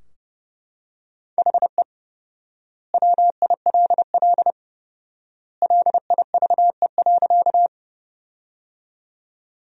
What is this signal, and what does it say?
Morse code, heard as a keyed tone, about 30 wpm, 700 Hz. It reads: HE WILL LIVE.
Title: 4x Word-Spacing - Sentences from Top 200 Words - 30wpm